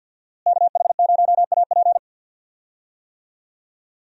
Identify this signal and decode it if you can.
Morse code, heard as a keyed tone, about 51 wpm, 700 Hz. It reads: KH0AP
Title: USA Call Signs - 5 Characters - Repeated 3x - 50wpm